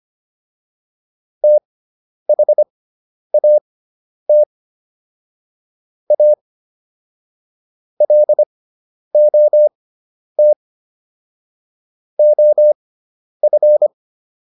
Morse code, heard as a keyed tone, about 25 wpm, spaced as for 10 wpm, 600 Hz. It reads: THAT A LOT OF